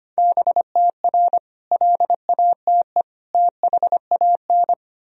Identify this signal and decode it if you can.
Morse code, heard as a keyed tone, about 25 wpm, 700 Hz. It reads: BTR LATE THAN
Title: Sentences from Top 400 Words and Abbreviations 25wpm